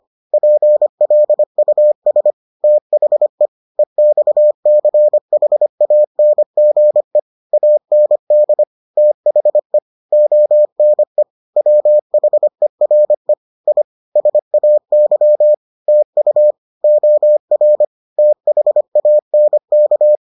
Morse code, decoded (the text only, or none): PLUS THE EXCHANGE AND THE ONE WHERE I SAY TU OR THANK